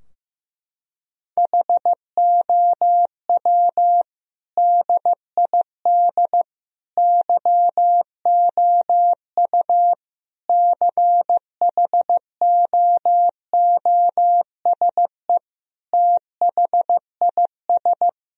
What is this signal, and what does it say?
Morse code, heard as a keyed tone, about 15 wpm, 700 Hz. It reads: HOW DID YOU CHOOSE THIS
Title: Sentences from Top 1000 Words - Review Part 4 - 15wpm